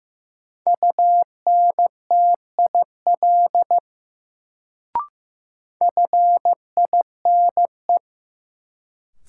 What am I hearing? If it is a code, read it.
Morse code, heard as a keyed tone, about 15 wpm, 700 Hz. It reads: UNTIL FINE
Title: Sentences from Top 500 Words - New Word Drill - 15wpm